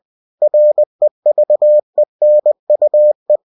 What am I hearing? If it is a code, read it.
Morse code, heard as a keyed tone, about 20 wpm, 600 Hz. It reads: REVENUE